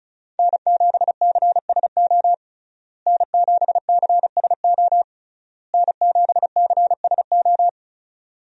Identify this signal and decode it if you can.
Morse code, heard as a keyed tone, about 36 wpm, 700 Hz. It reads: N7CSO N7CSO N7CSO
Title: USA Call Signs - 5 Characters - Repeated 3x - 35wpm